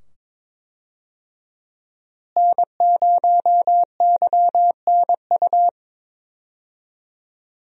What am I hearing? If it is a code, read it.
Morse code, heard as a keyed tone, about 22 wpm, 700 Hz. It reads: N0YNU
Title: USA Call Signs - 5 Characters - Repeated 2x - 22wpm